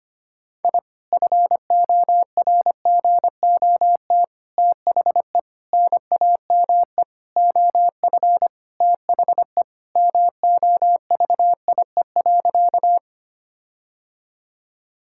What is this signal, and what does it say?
Morse code, heard as a keyed tone, about 25 wpm, 700 Hz. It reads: I FORGOT THE NAME OF THE MOVIE.
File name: practice / 25wpm / Sentences with 4-Word Phrases - 25wpm